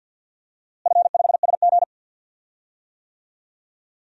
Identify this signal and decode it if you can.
Morse code, heard as a keyed tone, about 52 wpm, 700 Hz. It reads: F5SG
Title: DXCC-100 Call Signs - 4 Characters - Repeated 3x - 50wpm